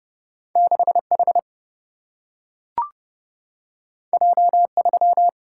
Morse code, heard as a keyed tone, about 30 wpm, 700 Hz. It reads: 6H J3
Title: Single-Letter Single-Number 30wpm